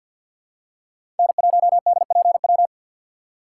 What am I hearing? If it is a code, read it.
Morse code, heard as a keyed tone, about 51 wpm, 700 Hz. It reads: N1DPW
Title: USA Call Signs - 5 Characters - 2x Character Spacing - 50wpm